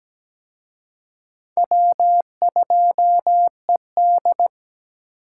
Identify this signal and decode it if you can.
Morse code, heard as a keyed tone, about 20 wpm, 700 Hz. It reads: W2ED
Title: Speed Racing - USA 1x2 Call Signs 17wpm